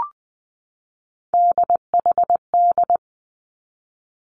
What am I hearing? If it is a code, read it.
Morse code, heard as a keyed tone, about 20 wpm, 700 Hz. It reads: DHD
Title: Three Challenging Letter-Number Combinations 20wpm